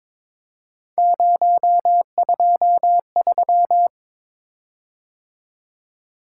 Morse code, heard as a keyed tone, about 23 wpm, 700 Hz. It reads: 023
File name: practice / 22wpm / Three Letter-Number Combinations - 22wpm